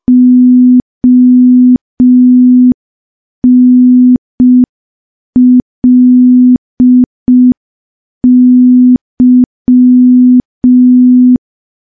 Morse code, heard as a keyed tone, about 5 wpm, 250 Hz. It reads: ONLY
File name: morse / morse-training-05WPM